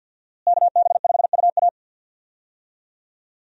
Morse code, heard as a keyed tone, about 52 wpm, 700 Hz. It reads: KB5UA
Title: USA Call Signs - 5 Characters - Repeated 3x - 50wpm